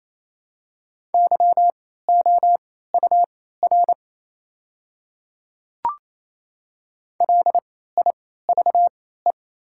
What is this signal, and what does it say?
Morse code, heard as a keyed tone, about 27 wpm, spaced as for 16 wpm, 690 Hz. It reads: YOUR LIVE
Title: 3x Character Spacing - Top 300 Words - 28wpm